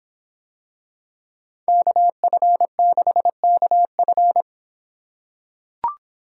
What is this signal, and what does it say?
Morse code, heard as a keyed tone, about 26 wpm, 700 Hz. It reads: KF6KF